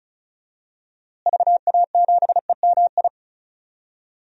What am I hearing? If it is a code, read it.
Morse code, heard as a keyed tone, about 36 wpm, 700 Hz. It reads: VA7EMI
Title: DXCC-100 Call Signs - 6 Characters - Repeated 2x - 35wpm